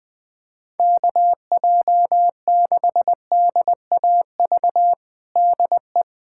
Morse code, heard as a keyed tone, about 20 wpm, 700 Hz. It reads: KJ6DAV DE
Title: USA Callsign DE Callsign - 6 Characters - 20wpm